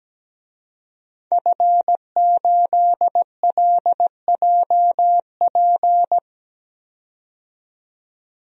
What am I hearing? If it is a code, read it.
Morse code, heard as a keyed tone, about 17 wpm, 700 Hz. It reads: F8LJP
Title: DXCC-100 Call Signs - 5 Characters - 17wpm